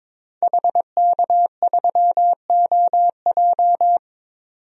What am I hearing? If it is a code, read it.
Morse code, heard as a keyed tone, about 22 wpm, 700 Hz. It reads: HK3OJ
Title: DXCC-100 Call Signs - 5 Characters - 22wpm